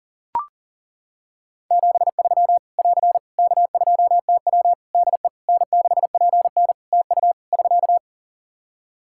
Morse code, heard as a keyed tone, about 40 wpm, 700 Hz. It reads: 73 <AR> K2TW DE N6PN TU <SK>
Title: Practice QSO - Exchange 4 - Ending - 40wpm